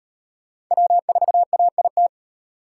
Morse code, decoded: W4AIT

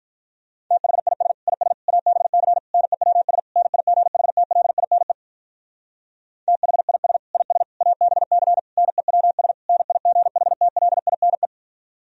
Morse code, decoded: THIS IS ABC NEWS NIGHTLINE THIS IS ABC NEWS NIGHTLINE

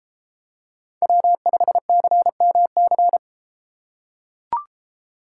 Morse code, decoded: W5CMC

